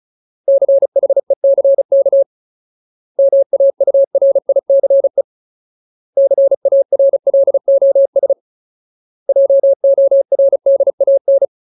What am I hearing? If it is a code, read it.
Morse code, heard as a keyed tone, about 35 wpm, 550 Hz. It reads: CHECK MAURICE CARLOS JORDAN
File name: Ultimate Word List / 35WPM Ultimate